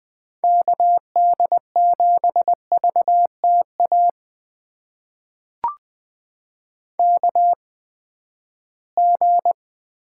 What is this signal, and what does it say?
Morse code, heard as a keyed tone, about 20 wpm, spaced as for 14 wpm, 700 Hz. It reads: KD7VTA KG
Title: USA Call Signs - 6 Characters - 8x Character Spacing - 20wpm